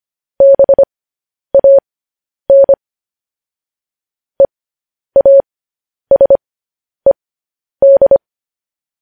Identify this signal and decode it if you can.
Morse code, heard as a keyed tone, about 25 wpm, spaced as for 10 wpm, 550 Hz. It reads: BAN EASED
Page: news headlines in morse code at 10 wpm